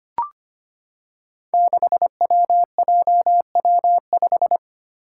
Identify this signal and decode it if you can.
Morse code, heard as a keyed tone, about 25 wpm, 700 Hz. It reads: 6WJW5
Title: Five Challenging Letter-Number Combinations 25wpm